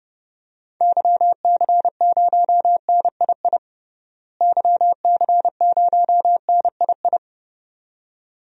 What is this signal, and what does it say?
Morse code, heard as a keyed tone, about 30 wpm, 700 Hz. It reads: YC0NII YC0NII
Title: DXCC-100 Call Signs - 6 Characters - Repeated 2x - 30wpm